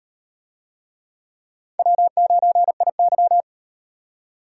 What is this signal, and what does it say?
Morse code, heard as a keyed tone, about 38 wpm, 700 Hz. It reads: W9IY